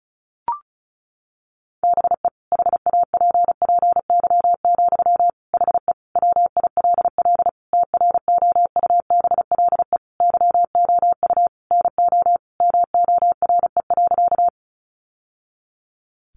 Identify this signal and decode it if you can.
Morse code, heard as a keyed tone, about 34 wpm, 700 Hz. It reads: BE HAPPY, HE WILL TROUBLE YOU NO MORE.
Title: Sentences from Top 1200 Words 35wpm